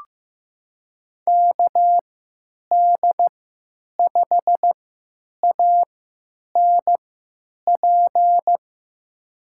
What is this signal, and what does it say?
Morse code, heard as a keyed tone, about 15 wpm, spaced as for 9 wpm, 700 Hz. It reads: KD5ANP
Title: USA Call Signs - 3x Character Spacing - 15wpm